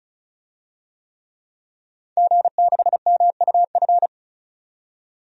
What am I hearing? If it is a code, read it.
Morse code, heard as a keyed tone, about 34 wpm, 700 Hz. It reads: G6MUF
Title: DXCC-100 Call Signs - 5 Characters - 35wpm